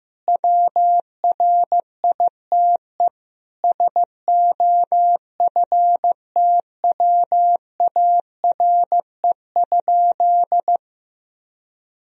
Morse code, heard as a keyed tone, about 14 wpm, 700 Hz. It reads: WRITE SOFTWARE?
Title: Sentences from Top 1000 Words - Review Part 3 - 15wpm